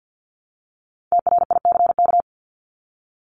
Morse code, heard as a keyed tone, about 50 wpm, 700 Hz. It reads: TRICK
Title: Top 1251-1500 Words 50wpm